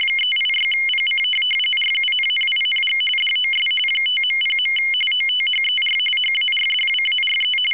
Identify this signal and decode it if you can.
RTTY, about 45.45 baud. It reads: 386PC RUNNING DOS 5.0. OTHER HARDWARE
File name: w ry850 / rtty45